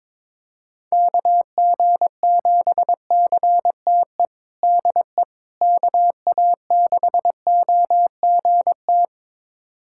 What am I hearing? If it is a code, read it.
Morse code, heard as a keyed tone, about 22 wpm, 700 Hz. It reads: KG7CTE DE KA6OGT
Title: USA Callsign DE Callsign - 6 Characters - 22wpm